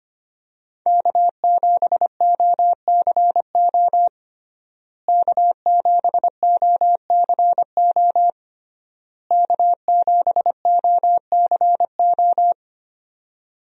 Morse code, heard as a keyed tone, about 25 wpm, 700 Hz. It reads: K7OCO K7OCO K7OCO
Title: USA Call Signs - 5 Characters - Repeated 3x - 25wpm